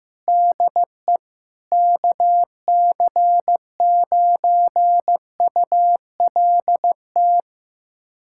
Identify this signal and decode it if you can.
Morse code, heard as a keyed tone, about 16 wpm, 700 Hz. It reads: DE KC9ULT
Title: USA Callsign DE Callsign - 6 Characters - 15wpm